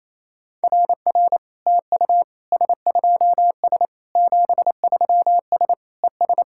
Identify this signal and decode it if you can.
Morse code, heard as a keyed tone, about 28 wpm, 700 Hz. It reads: RR TU S2S 73S ES
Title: SOTA QSO Practice 28wpm